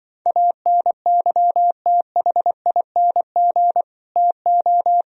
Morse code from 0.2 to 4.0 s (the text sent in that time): ANYTHING